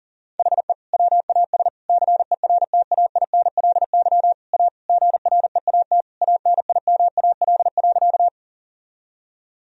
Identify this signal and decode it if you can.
Morse code, heard as a keyed tone, about 40 wpm, 700 Hz. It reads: HE WAS CERTAINLY A GREAT ANIMAL.
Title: Sentences from Top 1000 Words - Review Part 1 - 40wpm